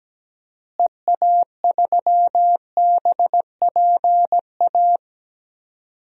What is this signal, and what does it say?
Morse code, heard as a keyed tone, about 17 wpm, 700 Hz. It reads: EA3BPA
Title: DXCC-100 Call Signs - 6 Characters - Repeated 2x - 17wpm